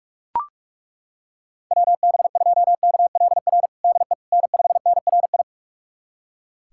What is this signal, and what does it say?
Morse code, heard as a keyed tone, about 42 wpm, 700 Hz. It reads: WB2KLR DE N5NRI
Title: USA Callsign DE Callsign 45wpm